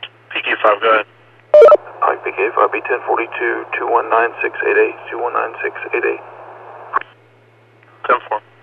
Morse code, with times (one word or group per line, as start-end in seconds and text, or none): none